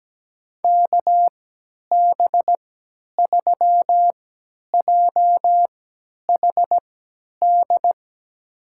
Morse code, heard as a keyed tone, about 17 wpm, spaced as for 10 wpm, 700 Hz. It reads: KB3JHD
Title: USA Call Signs - 6 Characters - 3x Character Spacing - 17wpm